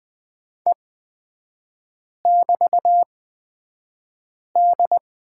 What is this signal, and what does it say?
Morse code, heard as a keyed tone, about 20 wpm, spaced as for 5 wpm, 700 Hz. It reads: E<BT>D
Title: 224 Lesson 36e - BT - Three Characters 20wpm